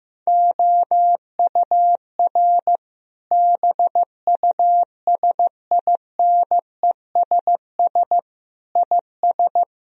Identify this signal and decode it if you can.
Morse code, heard as a keyed tone, about 16 wpm, 700 Hz. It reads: OUR BUSINESS IS
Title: Sentences from Top 1000 Words - Review Part 4 - 15wpm